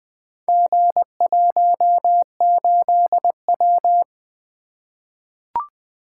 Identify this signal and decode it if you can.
Morse code, heard as a keyed tone, about 20 wpm, 700 Hz. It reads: G18W
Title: Four Letter-Number Combinations - 20wpm